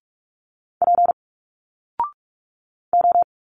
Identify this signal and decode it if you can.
Morse code, heard as a keyed tone, about 46 wpm, 700 Hz. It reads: P O
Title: Single Letters 45wpm